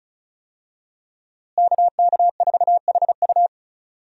Morse code, heard as a keyed tone, about 36 wpm, 700 Hz. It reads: KK4HU
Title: USA Call Signs - 5 Characters - Repeated 3x - 35wpm